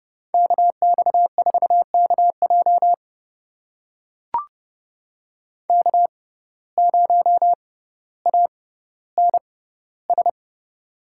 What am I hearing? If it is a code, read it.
Morse code, heard as a keyed tone, about 30 wpm, spaced as for 15 wpm, 700 Hz. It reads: KX4KJ K0ANS
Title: USA Call Signs - 5 Characters - 6x Character Spacing - 30wpm